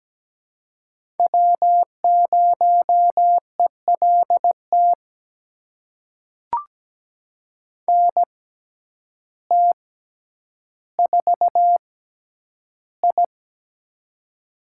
Morse code, heard as a keyed tone, about 17 wpm, spaced as for 9 wpm, 700 Hz. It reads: W0ELT NT4I
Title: USA Call Signs - 6x Character Spacing - 17wpm